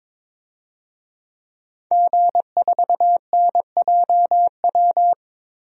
Morse code, heard as a keyed tone, about 22 wpm, 700 Hz. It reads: G4NJW
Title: DXCC-100 Call Signs - 5 Characters - 22wpm